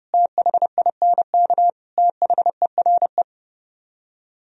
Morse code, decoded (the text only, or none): THINK THERE